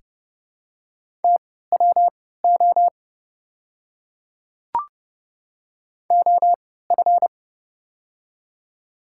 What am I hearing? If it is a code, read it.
Morse code, heard as a keyed tone, about 30 wpm, spaced as for 17 wpm, 700 Hz. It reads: TWO OF
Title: 3x Character Spacing - Top 100 Words - 30wpm